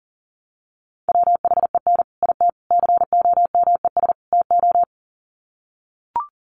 Morse code, heard as a keyed tone, about 40 wpm, 700 Hz. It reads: WHEN IT COMES TO
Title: Sets of 4 Words 40wpm